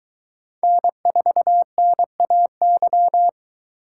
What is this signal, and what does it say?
Morse code, heard as a keyed tone, about 23 wpm, 700 Hz. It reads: N4NAY